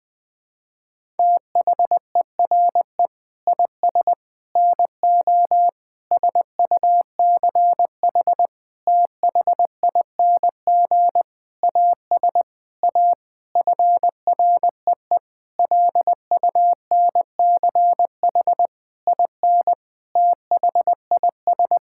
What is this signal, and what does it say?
Morse code, heard as a keyed tone, about 20 wpm, 700 Hz. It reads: THERE IS NO SUCH THING AS A FREE LUNCH IN THIS
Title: Sentences with 9-Word Phrases - 20wpm